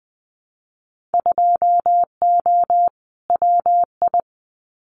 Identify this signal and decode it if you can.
Morse code, heard as a keyed tone, about 20 wpm, 700 Hz. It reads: 2O WI
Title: Winter Field Day Exchanges 20wpm